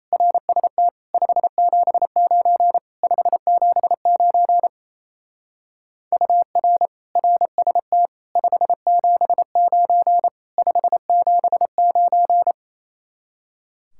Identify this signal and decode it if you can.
Morse code, heard as a keyed tone, about 34 wpm, 700 Hz. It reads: RST 579 579 UR RST 579 579
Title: Speed-Racing Common QSO Phrases - 28wpm